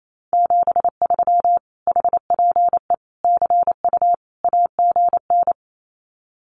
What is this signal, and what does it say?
Morse code, heard as a keyed tone, about 28 wpm, 700 Hz. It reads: 73 HPE CU AGN
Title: Common QSO Phrases Encore 28wpm